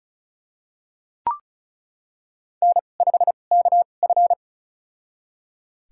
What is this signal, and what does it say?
Morse code, heard as a keyed tone, about 36 wpm, spaced as for 25 wpm, 700 Hz. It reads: N5KF
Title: USA Call Signs - 4 Characters - 2x Character Spacing - 35wpm